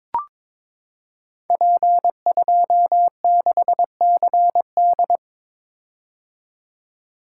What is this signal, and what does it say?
Morse code, heard as a keyed tone, about 22 wpm, 700 Hz. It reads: P26CD
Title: Code Groups - 5 Letter:Numbers - 22wpm